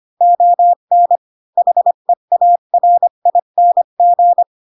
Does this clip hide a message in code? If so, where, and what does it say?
Yes: Morse code from 0.2 to 4.4 s, heard as a keyed tone, about 25 wpm, 700 Hz. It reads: ON HEARING